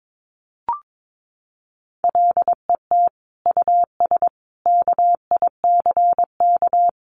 Morse code, heard as a keyed tone, about 22 wpm, 700 Hz. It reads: LET US KICK
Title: Sentences from Top 1300 Words 22wpm